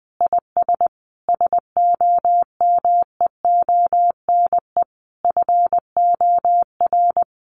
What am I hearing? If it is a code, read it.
Morse code, heard as a keyed tone, about 19 wpm, 700 Hz. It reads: IS SOMEONE FOR
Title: Sentences from Top 600 Words 20wpm